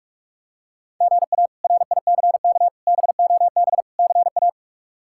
Morse code, heard as a keyed tone, about 46 wpm, 700 Hz. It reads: GA RICK BOB CA
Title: K1USN SST QSO Practice - 45wpm